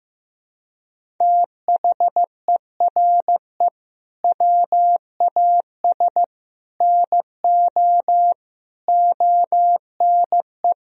Morse code, heard as a keyed tone, about 15 wpm, 700 Hz. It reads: THERE WAS NO ONE
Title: Sentences from Top 1000 Words - Review Part 4 - 15wpm